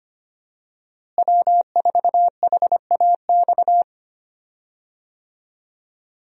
Morse code, heard as a keyed tone, about 25 wpm, 700 Hz. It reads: W4HAX
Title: USA Call Signs - 5 Characters - Repeated 3x - 25wpm